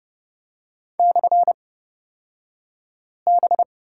Morse code, heard as a keyed tone, about 30 wpm, spaced as for 5 wpm, 700 Hz. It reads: /B
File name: Lesson 36d - BT - Two Characters 30wpm